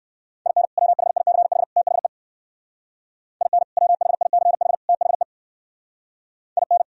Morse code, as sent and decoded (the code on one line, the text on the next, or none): .. -. / .-- .... .. -.-. .... / - .... / .. -. / .-- .... .. -.-. .... / - .... / .. -.
IN WHICH THE IN WHICH THE IN